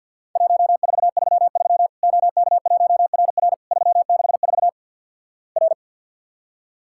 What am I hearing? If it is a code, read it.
Morse code, heard as a keyed tone, about 51 wpm, 700 Hz. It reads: 1433 OK1RR 364 R
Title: Worked All Europe DX Contest - QTC Exchange with Repeats - 50wpm